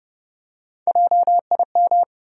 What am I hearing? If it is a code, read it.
Morse code, heard as a keyed tone, about 30 wpm, 700 Hz. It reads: JIM
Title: USA Common Names - Max 4 Letters - 30wpm